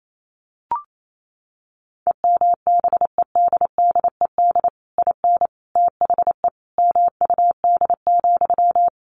Morse code, heard as a keyed tone, about 28 wpm, 700 Hz. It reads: EMBEDDED IN THE MUD,
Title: A Sound Of Thunder 28wpm